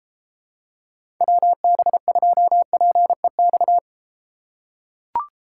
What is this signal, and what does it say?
Morse code, heard as a keyed tone, about 34 wpm, 700 Hz. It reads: WB2PEX